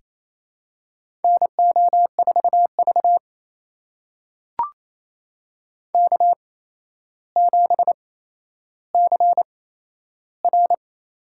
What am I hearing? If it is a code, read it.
Morse code, heard as a keyed tone, about 27 wpm, spaced as for 12 wpm, 700 Hz. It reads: NO4V K7CR
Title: USA Call Signs - 4 Characters - 8x Character Spacing - 28wpm